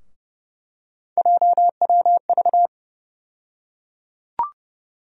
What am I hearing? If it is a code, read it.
Morse code, heard as a keyed tone, about 32 wpm, 700 Hz. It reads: JWV
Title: USA Call Sign Postfixes - 3 Letters 30wpm